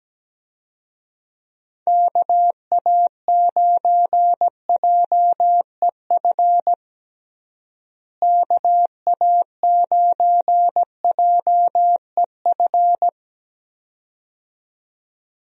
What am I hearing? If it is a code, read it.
Morse code, heard as a keyed tone, about 17 wpm, 700 Hz. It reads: KA9JEF KA9JEF
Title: USA Call Signs - 6 Characters - Repeated 2x - 17wpm